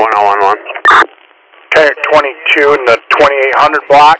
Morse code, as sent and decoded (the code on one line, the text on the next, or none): none